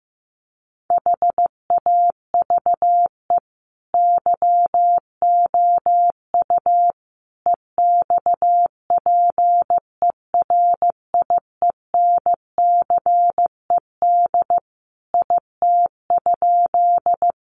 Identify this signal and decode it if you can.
Morse code, heard as a keyed tone, about 15 wpm, 700 Hz. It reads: HAVE YOU EXPERIENCED IT?
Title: Sentences from Top 1100 Words 15wpm